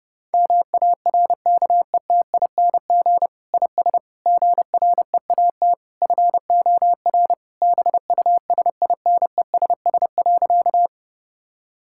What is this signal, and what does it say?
Morse code, heard as a keyed tone, about 30 wpm, 700 Hz. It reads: MARKETING IS GREAT FOR BUSINESS.
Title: Sentences from Top 1000 Words - Review Part 5 - 30wpm